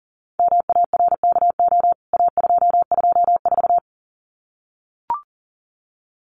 Morse code, heard as a keyed tone, about 40 wpm, 700 Hz. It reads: MARKO A224
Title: CWT Exchange 40wpm